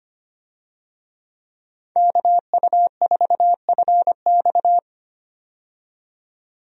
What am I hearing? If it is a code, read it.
Morse code, heard as a keyed tone, about 25 wpm, 700 Hz. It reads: KU4FX
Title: USA Call Signs - 5 Characters - 25wpm